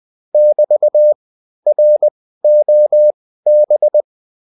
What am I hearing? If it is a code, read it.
Morse code, heard as a keyed tone, about 20 wpm, spaced as for 13 wpm, 600 Hz. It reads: <BT>ROB